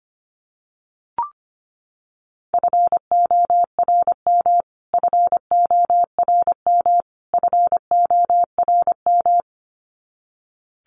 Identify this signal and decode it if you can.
Morse code, heard as a keyed tone, about 25 wpm, 700 Hz. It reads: FORM FORM FORM
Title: Top 500 Words Repeated 25wpm